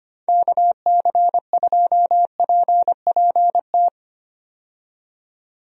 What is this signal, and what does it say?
Morse code, heard as a keyed tone, about 25 wpm, 700 Hz. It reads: KC2PPT
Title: USA Call Signs - 6 Characters - 25wpm